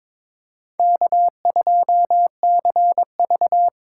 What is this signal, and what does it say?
Morse code, heard as a keyed tone, about 26 wpm, 700 Hz. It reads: K2CV
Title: Speed Racing - USA 1x2 Call Signs 22wpm